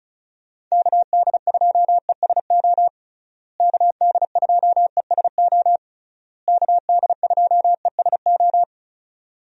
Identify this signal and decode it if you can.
Morse code, heard as a keyed tone, about 34 wpm, 700 Hz. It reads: KD2ESO KD2ESO KD2ESO
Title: USA Call Signs - 6 Characters - Repeated 3x - 35wpm